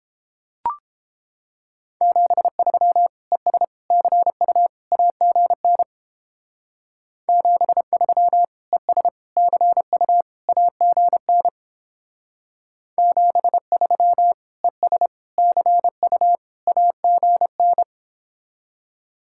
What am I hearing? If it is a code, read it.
Morse code, heard as a keyed tone, about 29 wpm, 700 Hz. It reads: 73 ES CU AGN 73 ES CU AGN 73 ES CU AGN